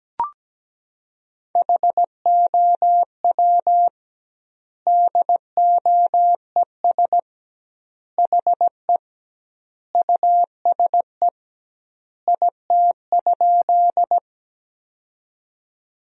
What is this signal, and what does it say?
Morse code, heard as a keyed tone, about 17 wpm, 700 Hz. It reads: HOW DOES HE USE IT?
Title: 2x Word-Spacing - Sentences from Top 200 Words - 17wpm